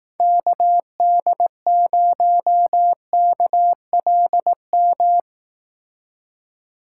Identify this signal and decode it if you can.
Morse code, heard as a keyed tone, about 18 wpm, 700 Hz. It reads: KD0KLM
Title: Speed Racing - USA 2x3 Call Signs 15wpm